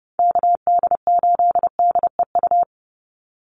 Morse code, heard as a keyed tone, about 30 wpm, 700 Hz. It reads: KD8DEU
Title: NA SOTA Chaser Callsigns 30wpm